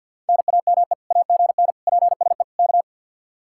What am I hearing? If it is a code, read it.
Morse code, heard as a keyed tone, about 67 wpm, spaced as for 58 wpm, 700 Hz. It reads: NAME AGN PSE K
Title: Speed-Racing Common QSO Phrases - 50wpm